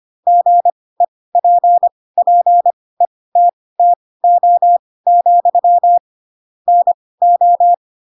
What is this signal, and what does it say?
Morse code, heard as a keyed tone, about 25 wpm, spaced as for 18 wpm, 700 Hz. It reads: GEPPETTO, NO
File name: Pinocchio ch3 12 25 ik1hge